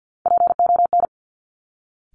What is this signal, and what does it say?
Morse code, heard as a keyed tone, about 50 wpm, 700 Hz. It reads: PON
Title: Top Three Letter Combinations in Words 50wpm